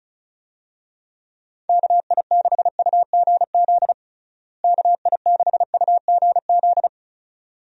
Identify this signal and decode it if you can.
Morse code, heard as a keyed tone, about 36 wpm, 700 Hz. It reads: KI6UGZ KI6UGZ
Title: USA Call Signs - 6 Characters - Repeated 2x - 35wpm